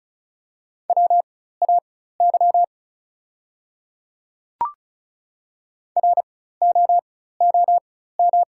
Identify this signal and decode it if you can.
Morse code, heard as a keyed tone, about 34 wpm, spaced as for 16 wpm, 700 Hz. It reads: WAY ROOM
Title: 4x Character Spacing - Top 500 Words - 35wpm